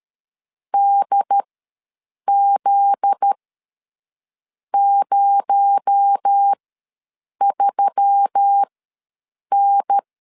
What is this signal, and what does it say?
Morse code, heard as a keyed tone, about 13 wpm, spaced as for 6 wpm, 800 Hz. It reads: DZ03N
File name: LEZIONI / 7 GF9Z / 7 GF9Z ws 12wpm 3321